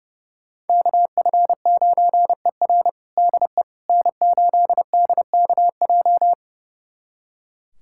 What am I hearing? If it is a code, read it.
Morse code, heard as a keyed tone, about 32 wpm, 700 Hz. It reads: KF9ER DE N8DKJ
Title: USA Callsign DE Callsign 30wpm